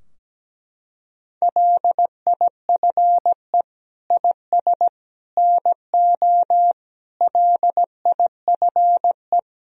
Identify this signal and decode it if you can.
Morse code, heard as a keyed tone, about 17 wpm, 700 Hz. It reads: LIFE IS NO LIFE
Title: Sentences from Top 1000 Words - Review Part 3 - 17wpm